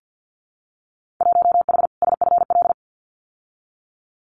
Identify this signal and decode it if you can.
Morse code, heard as a keyed tone, about 50 wpm, 700 Hz. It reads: 1H SFL
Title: Winter Field Day Exchanges 50wpm